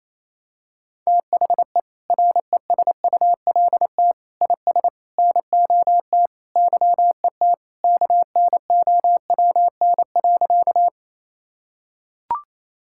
Morse code, heard as a keyed tone, about 28 wpm, 700 Hz. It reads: THE RESULT IS NOT YET KNOWN.
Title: Sentences from Top 1000 Words - Review Part 5 - 28wpm